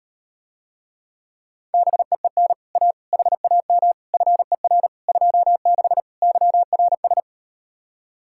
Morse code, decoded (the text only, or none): BEEN A HAM FER 26 YRS